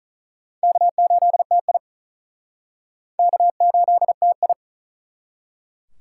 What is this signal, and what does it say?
Morse code, heard as a keyed tone, about 45 wpm, 700 Hz. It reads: K8TI K8TI